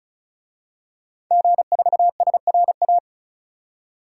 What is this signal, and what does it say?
Morse code, heard as a keyed tone, about 34 wpm, 700 Hz. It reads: G4SRA
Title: DXCC-100 Call Signs - 5 Characters - 35wpm